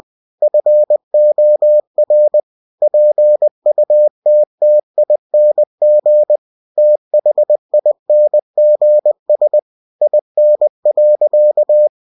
Morse code, heard as a keyed tone, about 20 wpm, 600 Hz. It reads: FOR PUTTING THINGS IN.
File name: cw-books / winnie-the-pooh / winnie-20wpm-page-0057